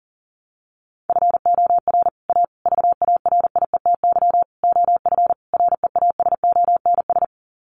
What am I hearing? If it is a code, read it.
Morse code, heard as a keyed tone, about 40 wpm, 700 Hz. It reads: FOR A VARIETY OF REASONS
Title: Sets of 5 Words Encore 40wpm